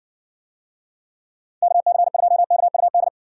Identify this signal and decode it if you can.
Morse code, heard as a keyed tone, about 70 wpm, 700 Hz. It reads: KC2KUD